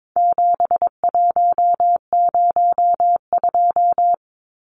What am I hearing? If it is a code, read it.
Morse code, heard as a keyed tone, about 22 wpm, 700 Hz. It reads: 7102
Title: Numbers - 4-Digit 22wpm